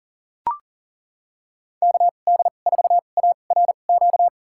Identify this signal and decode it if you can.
Morse code, heard as a keyed tone, about 37 wpm, spaced as for 28 wpm, 700 Hz. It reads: KD4ARQ
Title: USA Call Signs - 6 Characters - 2x Character Spacing - 40wpm